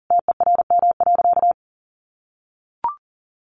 Morse code, heard as a keyed tone, about 40 wpm, 700 Hz. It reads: TERM.
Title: Sentences from Top 1100 Words 40wpm